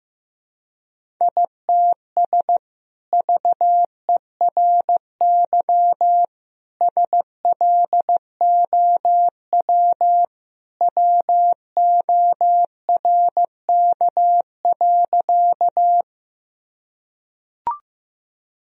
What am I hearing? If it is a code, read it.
Morse code, heard as a keyed tone, about 15 wpm, 700 Hz. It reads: ITS VERY SLOW WORK.
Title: Sentences from Top 1000 Words - Review Part 2 - 15wpm